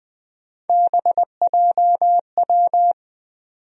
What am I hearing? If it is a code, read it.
Morse code, heard as a keyed tone, about 20 wpm, 700 Hz. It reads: BJW
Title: USA Call Sign Postfixes - 3 Letters 20wpm